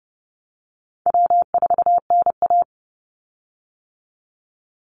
Morse code, heard as a keyed tone, about 30 wpm, 700 Hz. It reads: W4NA
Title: Easy Call Signs - 30wpm